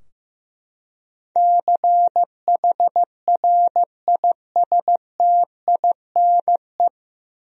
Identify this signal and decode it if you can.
Morse code, heard as a keyed tone, about 16 wpm, 700 Hz. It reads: CHRISTINE
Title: USA Common Names - 15wpm